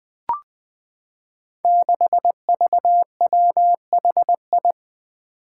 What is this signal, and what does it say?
Morse code, heard as a keyed tone, about 21 wpm, 700 Hz. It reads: 6VWHI
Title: Five Challenging Letter-Number Combinations 20wpm